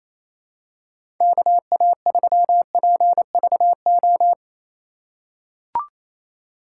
Morse code, heard as a keyed tone, about 28 wpm, 700 Hz. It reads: KA3PVO
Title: USA Call Signs - 6 Characters - 3x Character Spacing - 28wpm